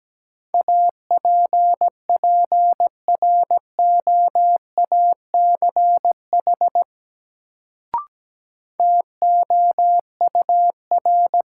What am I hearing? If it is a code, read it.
Morse code, heard as a keyed tone, about 17 wpm, 700 Hz. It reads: APPROACH TOUR
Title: Sentences from Top 1400 Words - New Word Drill Alt - 17wpm